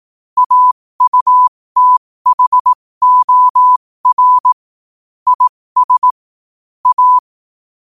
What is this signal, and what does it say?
Morse code, heard as a keyed tone, about 18 wpm, spaced as for 15 wpm, 1000 Hz. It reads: AUTHOR IS A
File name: qotdfiles / 15wpm